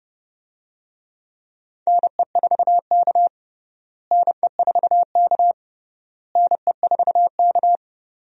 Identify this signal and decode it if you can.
Morse code, heard as a keyed tone, about 31 wpm, 700 Hz. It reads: NE4K NE4K NE4K
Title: USA Call Signs - 4 Characters - Repeated 3x - 30wpm